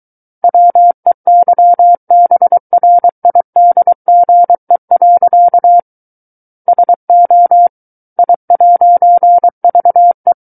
Morse code, heard as a keyed tone, about 23 wpm, 700 Hz. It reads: WEYBRIDGE. SO I'VE
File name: Book One / 14-Chapter 14 - 23 WPM